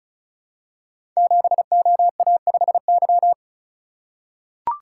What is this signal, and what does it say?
Morse code, heard as a keyed tone, about 36 wpm, 700 Hz. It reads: 7OA5Y
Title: Code Groups - 5 Letter:Numbers - 35wpm